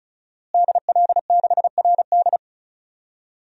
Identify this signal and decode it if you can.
Morse code, heard as a keyed tone, about 36 wpm, 700 Hz. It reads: DL6RD
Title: DXCC-100 Call Signs - 5 Characters - Repeated 2x - 35wpm